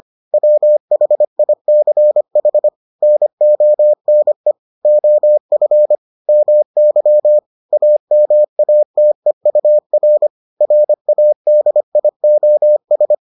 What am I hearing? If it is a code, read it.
Morse code, heard as a keyed tone, about 25 wpm, 600 Hz. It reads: WHICH NONE OF MY AMATEUR RADIOS